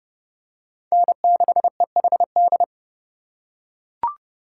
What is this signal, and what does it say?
Morse code, heard as a keyed tone, about 30 wpm, 700 Hz. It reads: N6EHD